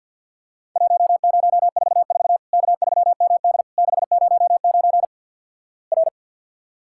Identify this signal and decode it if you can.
Morse code, heard as a keyed tone, about 50 wpm, 700 Hz. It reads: 1044 K3MD 609 R
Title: Worked All Europe DX Contest - QTC Exchange - 50wpm